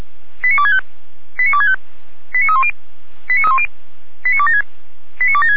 Five-tone selective call (ZVEI), heard as quick runs of five tones, tones of about 70 ms. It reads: 89265 89265 89310 89276 89276